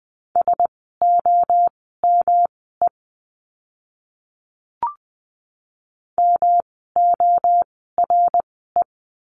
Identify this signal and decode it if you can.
Morse code, heard as a keyed tone, about 20 wpm, spaced as for 14 wpm, 700 Hz. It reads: SOME MORE
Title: 237 Lesson 38f - Letter q - Words 20wpm